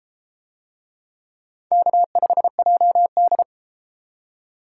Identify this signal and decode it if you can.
Morse code, heard as a keyed tone, about 32 wpm, 700 Hz. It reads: K5JD